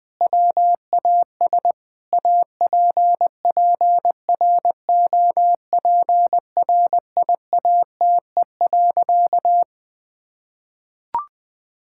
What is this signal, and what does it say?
Morse code, heard as a keyed tone, about 21 wpm, 700 Hz. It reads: WAS APPROPRIATE.
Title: Sentences from Top 1000 Words - Review Part 5 - 20wpm